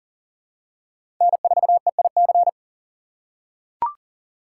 Morse code, heard as a keyed tone, about 40 wpm, 700 Hz. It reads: N4EIC
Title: USA Call Signs - 5 Characters - 2x Character Spacing - 40wpm